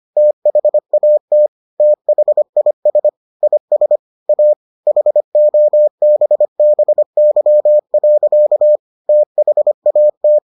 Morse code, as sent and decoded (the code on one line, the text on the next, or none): - .... .- - / - .... .. ... / .. ... / .- / .... --- -... -... -.-- .-.-.- / - .... .- -
THAT THIS IS A HOBBY. THAT